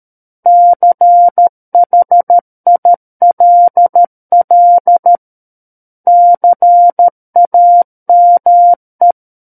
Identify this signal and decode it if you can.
Morse code, heard as a keyed tone, about 13 wpm, 700 Hz. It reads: CHILL CAME
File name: Book One / 04-Chapter 4 - 13 WPM